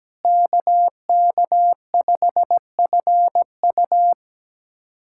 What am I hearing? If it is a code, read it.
Morse code, heard as a keyed tone, about 17 wpm, 700 Hz. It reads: KK5FU
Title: USA Callsign DE Callsign 17wpm